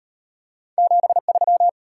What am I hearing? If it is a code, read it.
Morse code, heard as a keyed tone, about 38 wpm, 700 Hz. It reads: 73